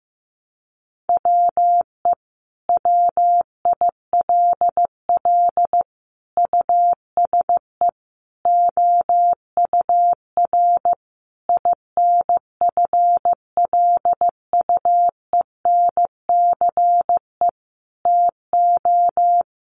Morse code, heard as a keyed tone, about 14 wpm, 700 Hz. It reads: WE WILL USE OUR INFLUENCE TO
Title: Sentences from Top 1100 Words 15wpm